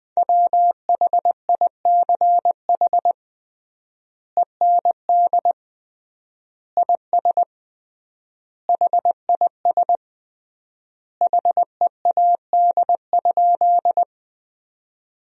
Morse code, heard as a keyed tone, about 20 wpm, 700 Hz. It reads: WHICH END IS HIS HEAD?
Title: 3x Word-Spacing - Sentences from Top 300 Words - 20wpm